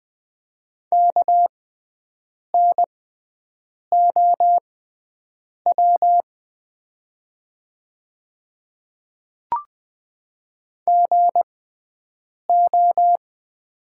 Morse code, heard as a keyed tone, about 20 wpm, spaced as for 7 wpm, 700 Hz. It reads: KNOW GO
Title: 6x Character Spacing - Top 100 Words - 20wpm